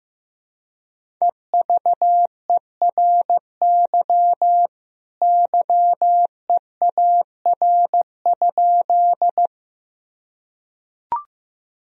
Morse code, heard as a keyed tone, about 15 wpm, 700 Hz. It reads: EVERY YEAR?
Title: Sentences from Top 1000 Words - Review Part 1 - 15wpm